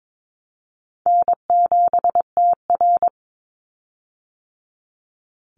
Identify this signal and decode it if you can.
Morse code, heard as a keyed tone, about 22 wpm, 700 Hz. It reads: N7TR
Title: Easy Call Signs - 22wpm